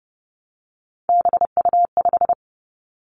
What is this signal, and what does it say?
Morse code, heard as a keyed tone, about 30 wpm, 700 Hz. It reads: BU5
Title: Three Challenging Letter-Number Combinations 30wpm